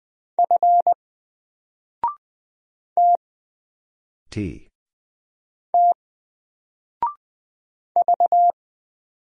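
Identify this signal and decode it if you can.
Morse code, heard as a keyed tone, about 20 wpm, 700 Hz. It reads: F T T V
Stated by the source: Single letter and number Morse drill at 20 wpm, from N4LCD.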